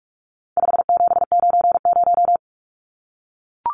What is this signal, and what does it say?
Morse code, heard as a keyed tone, about 44 wpm, 700 Hz. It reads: 5790
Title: Numbers - 4-Digit 45wpm